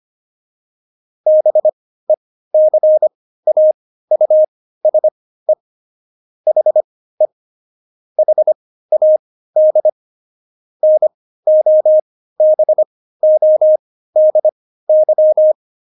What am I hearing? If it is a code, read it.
Morse code, heard as a keyed tone, about 25 wpm, spaced as for 15 wpm, 620 Hz. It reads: BECAUSE HE HAD NOBODY
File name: Jokes-25-char-15-eff